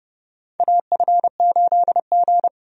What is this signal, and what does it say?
Morse code, heard as a keyed tone, about 30 wpm, 700 Hz. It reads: AF8G
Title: USA Call Signs - 4 Characters - 30wpm